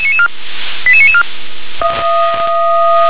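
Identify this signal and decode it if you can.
Five-tone selective call (ZVEI), heard as quick runs of five tones, tones of about 70 ms. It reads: 88994
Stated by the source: ZVEI five-tone call received on an SDR.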